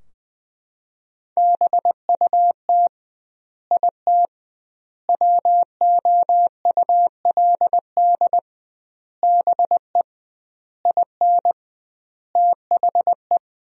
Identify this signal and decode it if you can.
Morse code, heard as a keyed tone, about 20 wpm, 700 Hz. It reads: BUT IT WOULD BE IN THE
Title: 2x Word-Spacing - Sentences from Top 300 Words - 20wpm